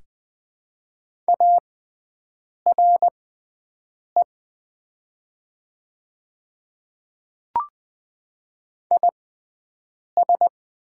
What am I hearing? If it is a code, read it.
Morse code, heard as a keyed tone, about 20 wpm, spaced as for 7 wpm, 700 Hz. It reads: ARE IS
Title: 6x Character Spacing - Top 200 Words - 20wpm